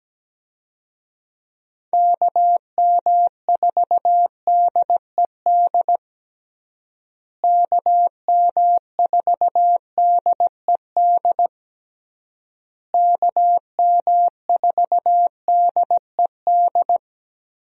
Morse code, heard as a keyed tone, about 17 wpm, 700 Hz. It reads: KM4DED KM4DED KM4DED
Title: USA Call Signs - 6 Characters - Repeated 3x - 17wpm